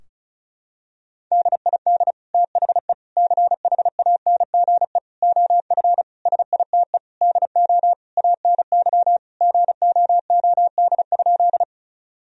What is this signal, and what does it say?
Morse code, heard as a keyed tone, about 36 wpm, 700 Hz. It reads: DID THE CHANGE OF SITE DO ANY GOOD?
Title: Sentences from Top 1000 Words - Review Part 4 - 35wpm